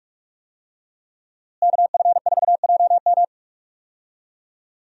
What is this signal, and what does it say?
Morse code, heard as a keyed tone, about 46 wpm, 700 Hz. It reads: KF4JM
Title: USA Call Signs - 5 Characters - Repeated 3x - 45wpm